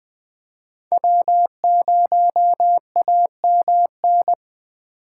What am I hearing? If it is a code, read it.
Morse code, heard as a keyed tone, about 21 wpm, 700 Hz. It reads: W0AMN
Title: USA Call Signs - 5 Characters - 4x Character Spacing - 20wpm